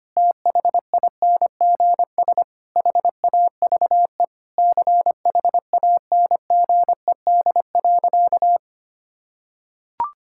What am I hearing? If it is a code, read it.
Morse code, heard as a keyed tone, about 25 wpm, 700 Hz. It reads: THINGS HAVE CHANGED.
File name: practice / 25wpm / Sentences with 3-Word Phrases - 25wpm